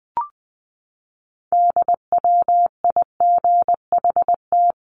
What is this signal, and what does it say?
Morse code, heard as a keyed tone, about 20 wpm, 700 Hz. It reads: DWIGHT
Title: CWT Exchange 20wpm